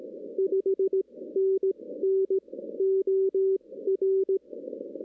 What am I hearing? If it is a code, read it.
Morse code, heard as a keyed tone, about 18 wpm, spaced as for 15 wpm, 380 Hz. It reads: 5NN OR
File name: 2016 arrl dx cw / 2016-ARRL-DX-CW-K5TR- - 0-25